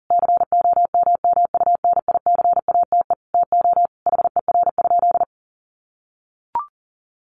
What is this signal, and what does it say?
Morse code, heard as a keyed tone, about 40 wpm, 700 Hz. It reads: COMMUNICATE TO HER?
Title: Sentences from Top 1300 Words 40wpm